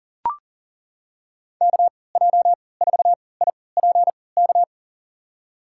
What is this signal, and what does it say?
Morse code, heard as a keyed tone, about 40 wpm, spaced as for 23 wpm, 700 Hz. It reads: KJ4IPK
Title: USA Call Signs - 6 Characters - 3x Character Spacing - 40wpm